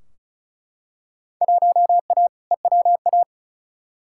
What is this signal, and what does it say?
Morse code, heard as a keyed tone, about 36 wpm, 690 Hz. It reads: 1A EWA
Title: Field Day Exchanges Encore - 35wpm